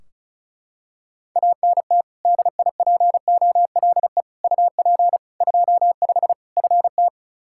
Morse code, heard as a keyed tone, about 36 wpm, 700 Hz. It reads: ANT DIPOLE UP 25 FT
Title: Practice QSO - Exchange 2 - Info - 35wpm